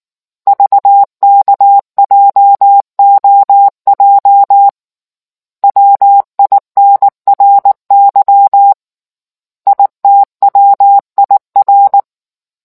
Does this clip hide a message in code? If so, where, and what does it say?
Yes: Morse code from 0.5 to 12.0 s, heard as a keyed tone, about 19 wpm, 800 Hz. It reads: VKJOJ WINRY ITWIR